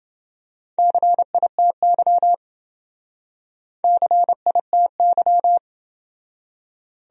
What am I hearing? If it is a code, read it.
Morse code, heard as a keyed tone, about 28 wpm, 700 Hz. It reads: CITY CITY